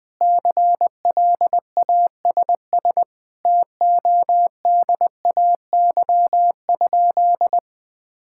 Morse code, heard as a keyed tone, about 20 wpm, 700 Hz. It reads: CLASS TODAY?
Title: Sentences from Top 1000 Words - Review Part 6 - 20wpm